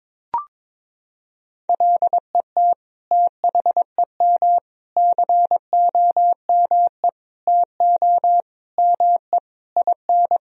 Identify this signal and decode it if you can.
Morse code, heard as a keyed tone, about 22 wpm, 700 Hz. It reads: LET THEM COME TO ME IN
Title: Sentences from Top 1000 Words - Review Part 3 - 22wpm